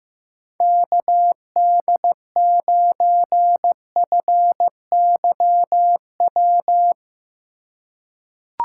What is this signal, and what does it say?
Morse code, heard as a keyed tone, about 15 wpm, 700 Hz. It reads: KD9FYW
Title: USA Call Signs - 6 Characters - 2x Character Spacing - 15wpm